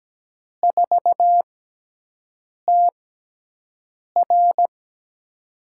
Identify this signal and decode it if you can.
Morse code, heard as a keyed tone, about 17 wpm, spaced as for 6 wpm, 700 Hz. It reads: 4TR
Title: USA Call Signs - 4 Characters - 6x Character Spacing - 17wpm